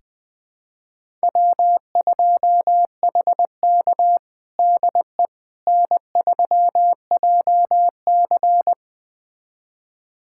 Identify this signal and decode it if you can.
Morse code, heard as a keyed tone, about 20 wpm, 700 Hz. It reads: W2HK DE N3JC
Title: USA Callsign DE Callsign - 4 Characters - 20wpm